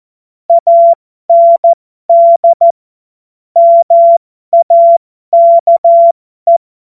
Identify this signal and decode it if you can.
Morse code, heard as a keyed tone, about 14 wpm, spaced as for 12 wpm, 680 Hz. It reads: AND MAKE
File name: Morsekurs / Witze